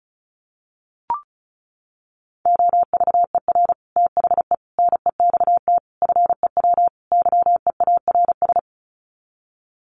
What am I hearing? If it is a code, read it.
Morse code, heard as a keyed tone, about 34 wpm, 700 Hz. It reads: OVER THE NEXT FEW YEARS
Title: Sets of 5 Words Encore 35wpm